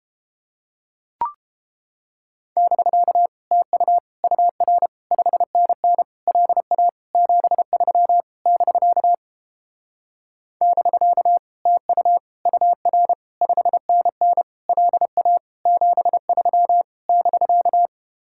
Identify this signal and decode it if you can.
Morse code, heard as a keyed tone, about 31 wpm, 700 Hz. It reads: <BK> TU UR 5NN LA 73 <BK> <BK> TU UR 5NN LA 73 <BK>